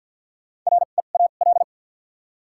Morse code, heard as a keyed tone, about 53 wpm, 700 Hz. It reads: R E A L
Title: 2x Character Spacing - Top 500 Words - 50wpm